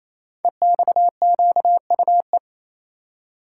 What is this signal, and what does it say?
Morse code, heard as a keyed tone, about 28 wpm, 700 Hz. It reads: EXQUE
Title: Code Groups - 5 Letter:Numbers - 28wpm